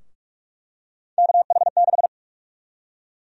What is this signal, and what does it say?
Morse code, heard as a keyed tone, about 42 wpm, 700 Hz. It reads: KH6
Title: USA Call Sign Prefixes - Repeated 2x - 45wpm